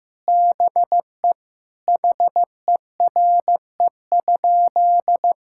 Morse code, heard as a keyed tone, about 16 wpm, 700 Hz. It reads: BE HERE?
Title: Sentences from Top 1000 Words - Review Part 5 - 15wpm